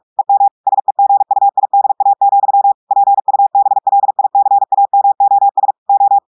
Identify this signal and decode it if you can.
Morse code, heard as a keyed tone, about 46 wpm, 800 Hz. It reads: EM SEGUIDA, PUBLICAMOS O
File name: portugais texte en clair 1 vitesse 45 mots minutes